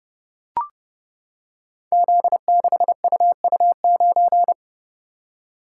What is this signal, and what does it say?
Morse code, heard as a keyed tone, about 29 wpm, 700 Hz. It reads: Z6UU9
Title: Code Groups - 5 Letter:Numbers - 30wpm